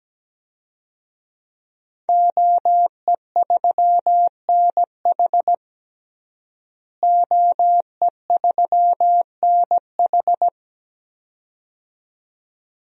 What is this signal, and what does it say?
Morse code, heard as a keyed tone, about 17 wpm, 700 Hz. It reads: OE3NH OE3NH
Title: DXCC-100 Call Signs - 5 Characters - Repeated 2x - 17wpm